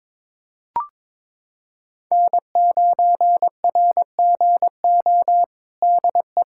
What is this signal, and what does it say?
Morse code, heard as a keyed tone, about 22 wpm, 700 Hz. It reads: N9RGO DE
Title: USA Callsign DE Callsign - 5 Characters - 22wpm